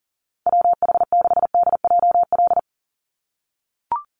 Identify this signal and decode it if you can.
Morse code, heard as a keyed tone, about 40 wpm, 700 Hz. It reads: WH6DJL
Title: USA Call Signs 40wpm